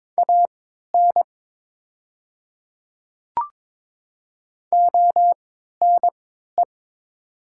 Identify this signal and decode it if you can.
Morse code, heard as a keyed tone, about 22 wpm, spaced as for 12 wpm, 700 Hz. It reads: AN ONE
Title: 3x Character Spacing - Top 100 Words - 22wpm